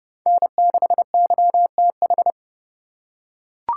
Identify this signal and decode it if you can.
Morse code, heard as a keyed tone, about 30 wpm, 700 Hz. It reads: N6YTH
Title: USA Call Signs - 5 Characters - 2x Character Spacing - 30wpm